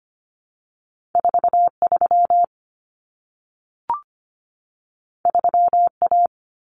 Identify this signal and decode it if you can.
Morse code, heard as a keyed tone, about 25 wpm, 700 Hz. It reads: 43 3A
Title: Common Cut Numbers - 2-Digit 25wpm